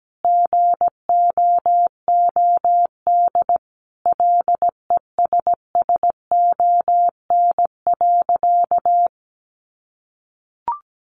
Morse code, heard as a keyed tone, about 17 wpm, 700 Hz. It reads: GOOD LESSON.
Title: Sentences from Top 1200 Words 17wpm